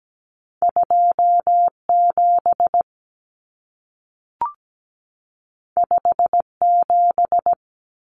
Morse code, heard as a keyed tone, about 17 wpm, 700 Hz. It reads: 27 57
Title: Numbers - 2-Digit 17wpm